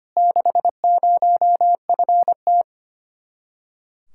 Morse code, heard as a keyed tone, about 24 wpm, 700 Hz. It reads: 60FT
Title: Speed-Racing Common QSO Phrases - 25wpm